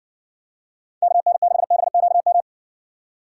Morse code, heard as a keyed tone, about 66 wpm, 700 Hz. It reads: KN6BQM